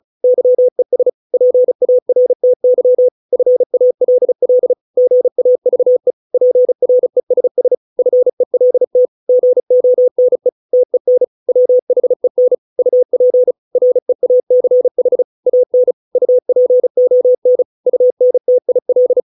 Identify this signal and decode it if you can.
Morse code, heard as a keyed tone, about 34 wpm, 500 Hz. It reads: YES PARTY FALL GAVE PRESS FELT GONE TEN WHEN UP REACH AN UPON UNTIL